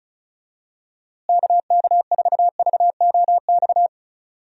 Morse code, heard as a keyed tone, about 36 wpm, 700 Hz. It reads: KK4VOX
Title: USA Call Signs - 6 Characters - Repeated 2x - 35wpm